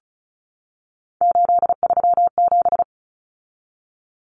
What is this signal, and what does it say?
Morse code, heard as a keyed tone, about 34 wpm, 700 Hz. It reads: 837
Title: Numbers - 3-Digit 35wpm